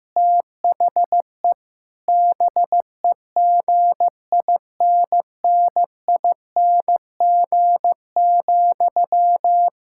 Morse code, heard as a keyed tone, about 15 wpm, 700 Hz. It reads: THE BEGINNING,
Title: Sentences from Top 1000 Words - Review Part 6 - 15wpm